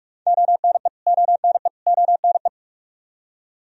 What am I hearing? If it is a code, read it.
Morse code, heard as a keyed tone, about 41 wpm, 700 Hz. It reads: ONE ONE ONE
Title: Top 100 Words Repeated 45wpm